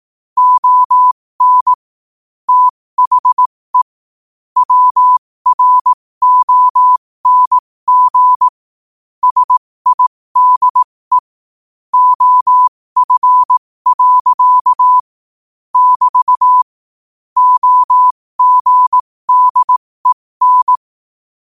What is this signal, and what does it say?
Morse code, heard as a keyed tone, about 18 wpm, spaced as for 15 wpm, 1000 Hz. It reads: ON THE WRONG SIDE OF. <BT> OGDEN
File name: qotdfiles / 15wpm